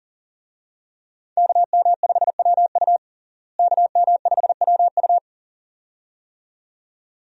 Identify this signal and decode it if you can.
Morse code, heard as a keyed tone, about 40 wpm, 700 Hz. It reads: KM5WU KM5WU
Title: USA Call Signs - 5 Characters - Repeated 2x - 40wpm